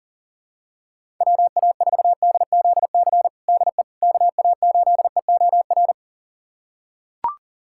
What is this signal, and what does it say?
Morse code, heard as a keyed tone, about 40 wpm, 700 Hz. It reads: WA4DZC DE KA8EOR
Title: USA Callsign DE Callsign - 6 Characters - 40wpm